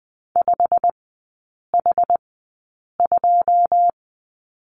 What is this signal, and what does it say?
Morse code, heard as a keyed tone, about 20 wpm, spaced as for 8 wpm, 700 Hz. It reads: 5H2
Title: 107 Lesson 18e - Letter m - Three Characters 20wpm